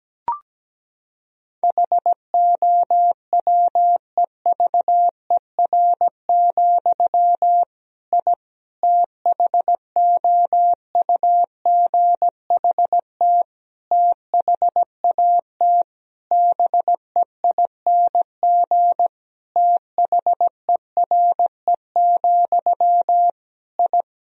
Morse code, heard as a keyed tone, about 17 wpm, 700 Hz. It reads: HOWEVER, I THOUGHT THAT BEING THERE, I
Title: Sentences from Top 1000 Words - Review Part 3 - 17wpm